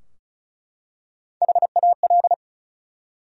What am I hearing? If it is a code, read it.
Morse code, heard as a keyed tone, about 36 wpm, 700 Hz. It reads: HAL
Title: USA Common Names - Max 4 Letters - 35wpm